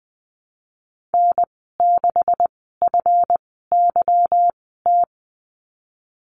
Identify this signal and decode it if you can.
Morse code, heard as a keyed tone, about 20 wpm, spaced as for 15 wpm, 700 Hz. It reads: N6FYT
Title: 154 Lesson 25g - Letter p - Call Signs 20wpm